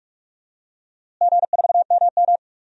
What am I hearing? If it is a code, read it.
Morse code, heard as a keyed tone, about 46 wpm, 700 Hz. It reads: G4MM
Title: DXCC-100 Call Signs - 4 Characters - Repeated 2x - 45wpm